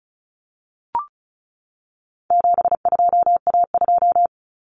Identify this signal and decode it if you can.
Morse code, heard as a keyed tone, about 34 wpm, 700 Hz. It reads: 72A2
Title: Common Cut Numbers - 4-Digit 35wpm